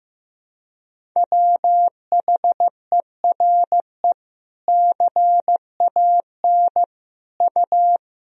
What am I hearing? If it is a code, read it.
Morse code, heard as a keyed tone, about 15 wpm, 700 Hz. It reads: WHERE CAN U
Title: Sentences from Top 500 Words and Abbreviations 15wpm